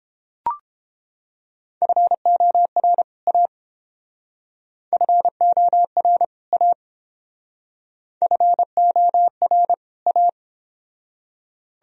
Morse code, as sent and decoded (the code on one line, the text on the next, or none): ..-. --- .-. / .- / ..-. --- .-. / .- / ..-. --- .-. / .-
FOR A FOR A FOR A